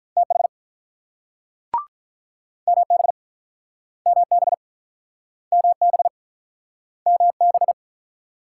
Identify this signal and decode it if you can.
Morse code, heard as a keyed tone, about 45 wpm, 700 Hz. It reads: TH MB MB MB MB